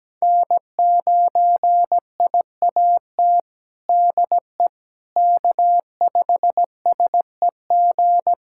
Morse code, heard as a keyed tone, about 17 wpm, 700 Hz. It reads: N9IAT DE K5SEG
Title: USA Callsign DE Callsign - 5 Characters - 17wpm